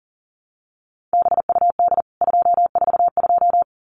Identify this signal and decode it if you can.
Morse code, heard as a keyed tone, about 40 wpm, 700 Hz. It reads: BUD 242
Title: CWT Exchange 40wpm